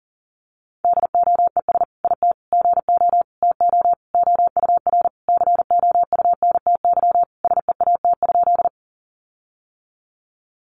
Morse code, heard as a keyed tone, about 40 wpm, 700 Hz. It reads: DOES IT GO TO OUR COUNTY SEAT?
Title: Sentences from Top 1500 Words 40wpm